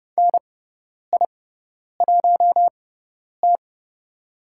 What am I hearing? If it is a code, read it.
Morse code, heard as a keyed tone, about 30 wpm, spaced as for 10 wpm, 700 Hz. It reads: NI1T
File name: Lesson 17g - Letter u - Call Signs 30wpm